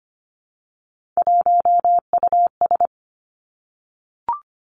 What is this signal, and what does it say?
Morse code, heard as a keyed tone, about 25 wpm, 700 Hz. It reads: 1US
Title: Three Challenging Letter-Number Combinations 25wpm